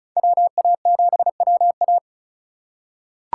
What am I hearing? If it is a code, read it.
Morse code, heard as a keyed tone, about 36 wpm, 690 Hz. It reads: WA7WA
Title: USA Call Signs - 5 Characters - 3x Character Spacing - 35wpm